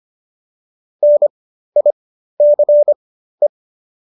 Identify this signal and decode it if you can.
Morse code, heard as a keyed tone, about 25 wpm, spaced as for 13 wpm, 590 Hz. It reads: NICE